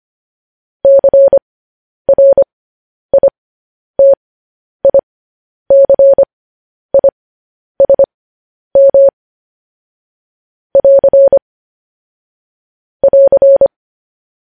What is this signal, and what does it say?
Morse code, heard as a keyed tone, about 25 wpm, spaced as for 10 wpm, 550 Hz. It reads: CRITICISM <AR> <AR>
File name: podcast / News headlines-1x-25x10